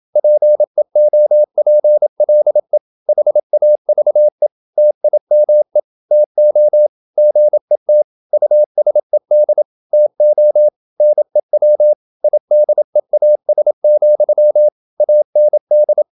Morse code, heard as a keyed tone, about 27 wpm, 600 Hz. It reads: PEOPLE HAVE TIME TO GET USED TO NEW IDEAS, AND
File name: Never 0027wpm